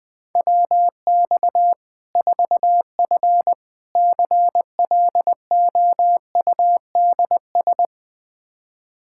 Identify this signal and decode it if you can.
Morse code, heard as a keyed tone, about 20 wpm, 700 Hz. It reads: WX 4F CLOUDS
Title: Practice QSO - Exchange 2 - Info - 20wpm